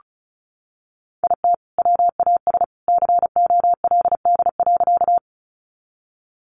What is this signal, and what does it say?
Morse code, heard as a keyed tone, about 34 wpm, 700 Hz. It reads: IT WAS COLD.
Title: Sentences from Top 400 Words 35wpm